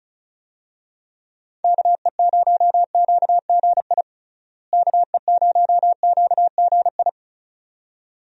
Morse code, heard as a keyed tone, about 34 wpm, 700 Hz. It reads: KE0QGI KE0QGI
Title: USA Call Signs - 6 Characters - Repeated 2x - 35wpm